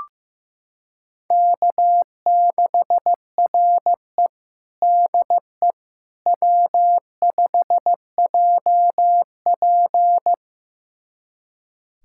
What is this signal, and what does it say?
Morse code, heard as a keyed tone, about 16 wpm, 700 Hz. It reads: K6RE DE W5JP
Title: USA Callsign DE Callsign - 4 Characters - 15wpm